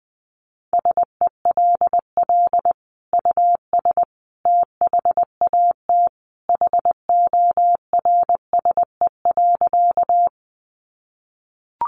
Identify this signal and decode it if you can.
Morse code, heard as a keyed tone, about 20 wpm, 700 Hz. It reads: SELL US THAT HORSE.
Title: Sentences from Top 1100 Words 20wpm